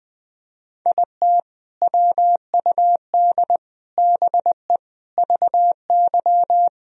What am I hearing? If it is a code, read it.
Morse code, heard as a keyed tone, about 21 wpm, 700 Hz. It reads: IT WUD BE VY
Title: Sentences from Top 200 Words and Abbreviations 20wpm